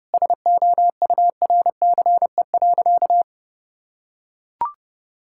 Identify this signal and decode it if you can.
Morse code, heard as a keyed tone, about 30 wpm, 700 Hz. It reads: SOURCE.
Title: Sentences from Top 1000 Words - Review Part 5 - 30wpm